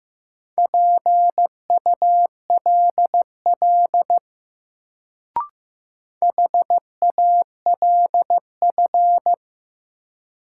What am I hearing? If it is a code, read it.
Morse code, heard as a keyed tone, about 15 wpm, 700 Hz. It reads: PULL HALF
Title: Sentences from Top 400 Words - New Word Drill - 15wpm